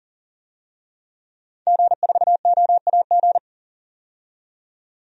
Morse code, heard as a keyed tone, about 37 wpm, 700 Hz. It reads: G4OAG
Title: DXCC-100 Call Signs - 5 Characters - 40wpm